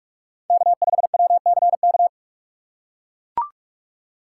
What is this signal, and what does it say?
Morse code, heard as a keyed tone, about 44 wpm, 690 Hz. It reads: K5WCK
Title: USA Call Signs - 5 Characters - 6x Character Spacing - 45wpm